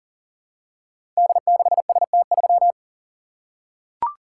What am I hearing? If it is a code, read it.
Morse code, heard as a keyed tone, about 39 wpm, 700 Hz. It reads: D6ST3
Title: Code Groups - 5 Letter:Numbers - 40wpm